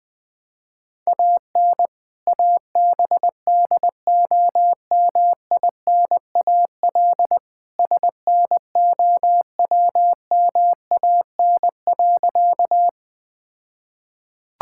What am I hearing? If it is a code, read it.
Morse code, heard as a keyed tone, about 21 wpm, 700 Hz. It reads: AN ABDOMINAL SNOWMAN.
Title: All-Age Appropriate Jokes - 20wpm